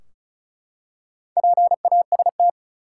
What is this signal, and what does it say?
Morse code, heard as a keyed tone, about 34 wpm, 700 Hz. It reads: PAST
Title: Four Letter Words - 35wpm